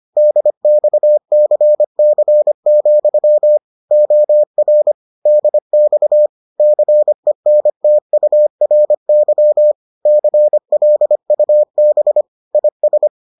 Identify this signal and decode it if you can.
Morse code, heard as a keyed tone, about 25 wpm, 600 Hz. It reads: DXCC, OR DX CENTURY CLUB IS